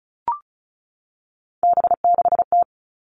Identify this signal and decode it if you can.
Morse code, heard as a keyed tone, about 36 wpm, 700 Hz. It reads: B6T
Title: Three Challenging Letter-Number Combinations 35wpm